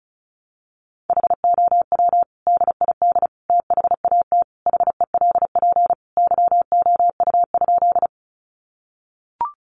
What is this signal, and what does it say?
Morse code, heard as a keyed tone, about 34 wpm, 700 Hz. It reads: HOW DID THAT HELP YOU?
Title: Sentences from Top 200 Words 35wpm